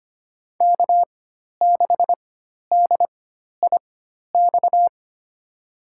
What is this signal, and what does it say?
Morse code, heard as a keyed tone, about 25 wpm, spaced as for 12 wpm, 700 Hz. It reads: K6DIX
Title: USA Call Signs - 5 Characters - 4x Character Spacing - 25wpm